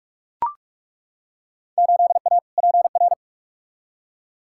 Morse code, heard as a keyed tone, about 43 wpm, 700 Hz. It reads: 8A PR
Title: Field Day Exchanges Encore - 45wpm